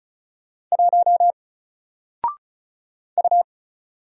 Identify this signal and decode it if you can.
Morse code, heard as a keyed tone, about 33 wpm, 700 Hz. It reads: 1 U
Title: Single Letter-Number 35wpm